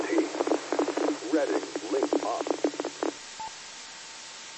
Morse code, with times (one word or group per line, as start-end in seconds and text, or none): none